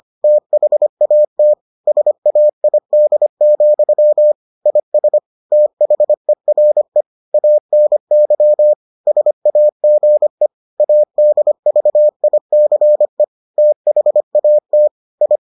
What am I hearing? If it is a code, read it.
Morse code, heard as a keyed tone, about 25 wpm, 600 Hz. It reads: THAT SAID, IS THERE ANY SAGE ADVICE THAT I